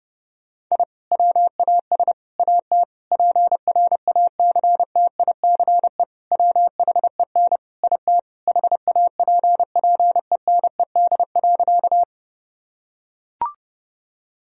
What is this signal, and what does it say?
Morse code, heard as a keyed tone, about 30 wpm, 700 Hz. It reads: I WAS AT PRACTICE WHEN IT HAPPENED.
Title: Sentences from Top 1000 Words - Review Part 4 - 30wpm